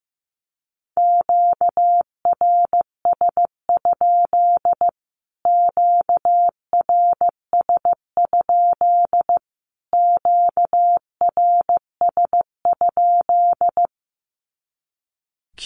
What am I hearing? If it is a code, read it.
Morse code, heard as a keyed tone, about 15 wpm, 700 Hz. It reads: QRS? QRS? QRS?
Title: Q-Codes Repeated 3x 15wpm